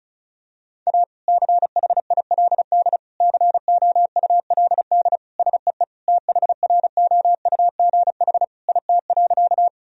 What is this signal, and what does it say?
Morse code, heard as a keyed tone, about 36 wpm, 700 Hz. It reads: A CHILD COULD SEE THROUGH IT.
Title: Sentences from Top 1000 Words - Review Part 1 - 35wpm